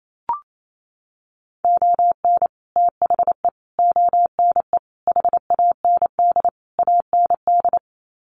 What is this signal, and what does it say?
Morse code, heard as a keyed tone, about 28 wpm, 700 Hz. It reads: ON THE ONE HAND AND
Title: Sets of 5 Words Encore2 28wpm